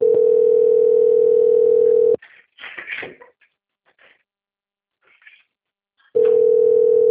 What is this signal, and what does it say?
Ringback tone (North America), 440 + 480 Hz.
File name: out- - - - - .34175